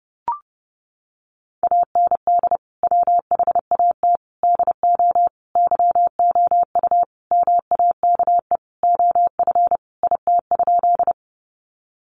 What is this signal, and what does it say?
Morse code, heard as a keyed tone, about 30 wpm, 700 Hz. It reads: AND WHAT DO YOU MAKE OF IT?
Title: Sentences from Top 100 Words 30wpm